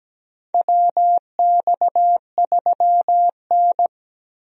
Morse code, heard as a keyed tone, about 17 wpm, 700 Hz. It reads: WX3N
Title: USA Call Signs - 4 Characters - 17wpm